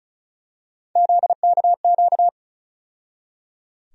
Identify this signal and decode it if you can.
Morse code, heard as a keyed tone, about 36 wpm, 700 Hz. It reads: ZKQ
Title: USA Call Sign Postfixes - 3 Letters 35wpm